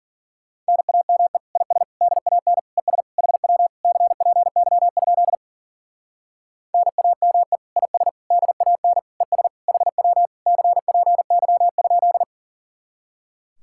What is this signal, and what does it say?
Morse code, heard as a keyed tone, about 51 wpm, 700 Hz. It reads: NAME IS DAN ES HW CPY? NAME IS DAN ES HW CPY?